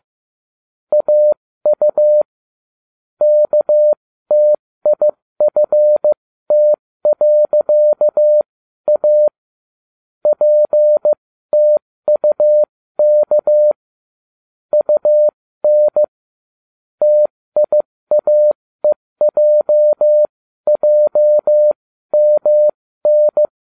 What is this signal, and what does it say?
Morse code, heard as a keyed tone, about 15 wpm, spaced as for 13 wpm, 600 Hz. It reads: AU KTIFT.A PTUK UN TIAEJJMN